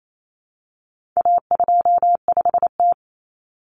Morse code, heard as a keyed tone, about 28 wpm, 700 Hz. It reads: A25T
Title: Common Cut Numbers - 4-Digit 28wpm